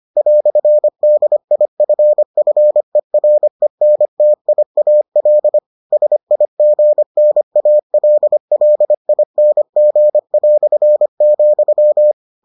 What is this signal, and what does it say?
Morse code, heard as a keyed tone, about 25 wpm, 600 Hz. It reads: "DIFFERENTIAL SIGNALLING",